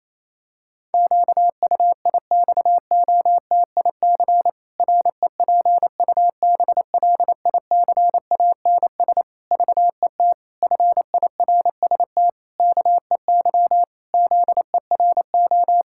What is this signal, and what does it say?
Morse code, heard as a keyed tone, about 28 wpm, 700 Hz. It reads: QUIXOTIC REPUBLICANS VET FIRST KEY ZERO
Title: Pangram Challenge v2 - 28wpm